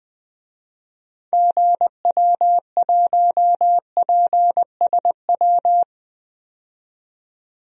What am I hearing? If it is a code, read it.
Morse code, heard as a keyed tone, about 21 wpm, 700 Hz. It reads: GW1PSW
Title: DXCC-100 Call Signs - 6 Characters - 20wpm